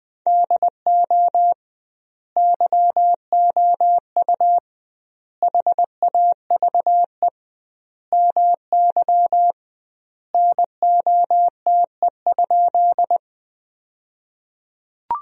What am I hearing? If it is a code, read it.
Morse code, heard as a keyed tone, about 20 wpm, 700 Hz. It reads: DO YOU HAVE MY NOTE?
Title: 2x Word-Spacing - Sentences from Top 500 Words - 20wpm